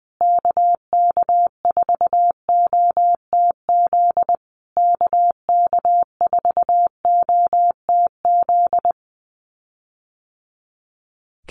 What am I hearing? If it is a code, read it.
Morse code, heard as a keyed tone, about 20 wpm, 700 Hz. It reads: KK4OTZ KK4OTZ
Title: USA Call Signs Repeated 3x 20wpm